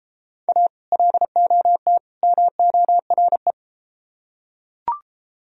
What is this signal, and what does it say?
Morse code, heard as a keyed tone, about 33 wpm, 700 Hz. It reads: A LOT MORE